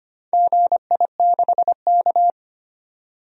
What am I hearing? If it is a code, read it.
Morse code, heard as a keyed tone, about 25 wpm, 700 Hz. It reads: GI6K
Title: DXCC-100 Call Signs - 4 Characters - 25wpm